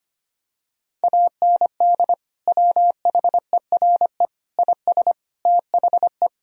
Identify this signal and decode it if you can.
Morse code, heard as a keyed tone, about 25 wpm, 700 Hz. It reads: AND WHERE IS THE
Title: Sentences from Top 1000 Words - Review Part 3 - 25wpm